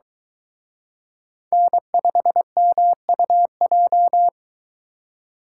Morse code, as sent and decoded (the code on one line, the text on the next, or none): -. ..... -- ..- .---
N5MUJ